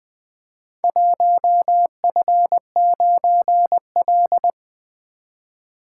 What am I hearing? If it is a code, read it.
Morse code, heard as a keyed tone, about 20 wpm, 700 Hz. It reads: 1F9L
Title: Four Letter-Number Combinations - 20wpm